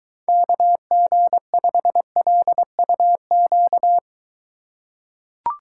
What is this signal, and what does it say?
Morse code, heard as a keyed tone, about 22 wpm, 700 Hz. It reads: KG5LUQ